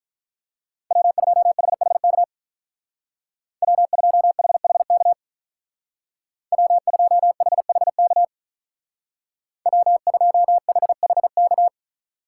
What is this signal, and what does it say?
Morse code, heard as a keyed tone, about 43 wpm, 700 Hz. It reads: W2HHK W2HHK W2HHK W2HHK
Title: Speed Racing - USA 1x3 Call Signs 35wpm